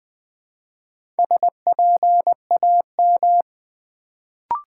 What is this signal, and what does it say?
Morse code, heard as a keyed tone, about 20 wpm, 700 Hz. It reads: SPAM
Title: Build A Word - 1 Letter At A Time - 4-Letter Words - 20wpm